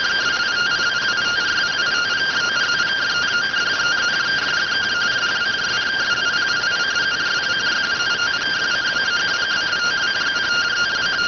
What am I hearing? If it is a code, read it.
Two-tone teleprinter signal, about 170 Hz shift.